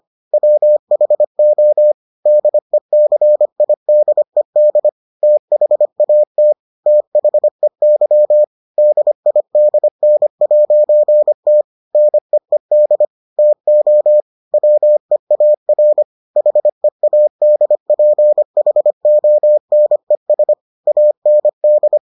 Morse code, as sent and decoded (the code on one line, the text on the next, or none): .-- .... --- / -.. -.-. .. -.. -.. / - .... .- - / - .... -.-- / -.. .. -.. -. .----. - / -. . -.. / - --- / .-- . .- .-. / .... .- -.. .--. .... --- -. ... / .- -. -..
WHO DECIDED THAT THEY DIDN'T NEED TO WEAR HEADPHONES AND